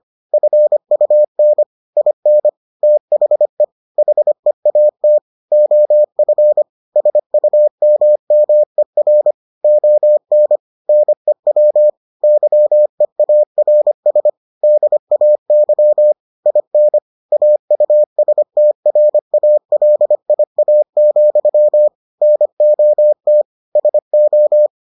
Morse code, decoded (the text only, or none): FUN IN THE HEAT OF SUMMER ON NEW YEARS DAY IN AUSTRALIA, NOT SO